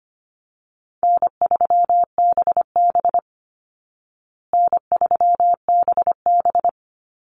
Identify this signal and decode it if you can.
Morse code, heard as a keyed tone, about 25 wpm, 700 Hz. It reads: N3BB N3BB
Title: Easy Call Signs Repeated 2x - 25wpm